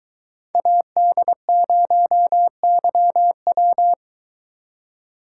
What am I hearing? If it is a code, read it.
Morse code, heard as a keyed tone, about 22 wpm, 700 Hz. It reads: AD0YW